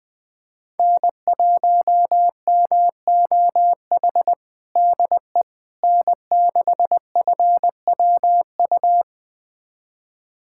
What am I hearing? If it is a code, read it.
Morse code, heard as a keyed tone, about 21 wpm, 700 Hz. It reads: N1MOH DE N6FWU
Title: USA Callsign DE Callsign - 5 Characters - 20wpm